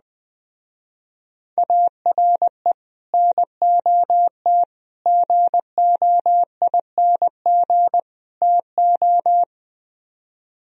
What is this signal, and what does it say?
Morse code, heard as a keyed tone, about 20 wpm, 700 Hz. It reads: ARE NOT GOING TO
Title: Speed-Racing - Sets of 4 Words 17wpm